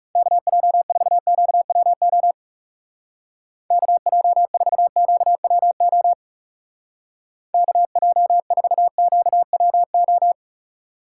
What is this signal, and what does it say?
Morse code, heard as a keyed tone, about 39 wpm, 700 Hz. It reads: KJ4QWO KJ4QWO KJ4QWO